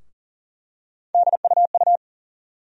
Morse code, heard as a keyed tone, about 40 wpm, 700 Hz. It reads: DUU
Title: USA Call Sign Postfixes - 3 Letters 40wpm